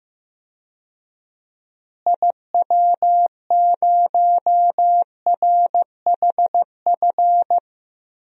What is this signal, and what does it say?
Morse code, heard as a keyed tone, about 16 wpm, 700 Hz. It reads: IW0RHF
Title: DXCC-100 Call Signs - 6 Characters - Repeated 3x - 15wpm